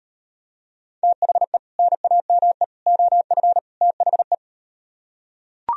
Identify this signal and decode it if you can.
Morse code, heard as a keyed tone, about 38 wpm, 700 Hz. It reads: THE NAME OF THE